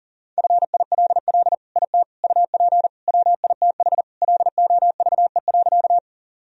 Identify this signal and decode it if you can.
Morse code, heard as a keyed tone, about 40 wpm, 700 Hz. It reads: FILL IT UP WITH LOVE.
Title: Sentences from Top 1000 Words - Review Part 3 - 40wpm